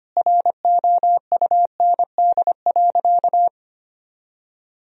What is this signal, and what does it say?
Morse code, heard as a keyed tone, about 25 wpm, 700 Hz. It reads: ROUND.
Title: Sentences from Top 1000 Words - Review Part 1 - 25wpm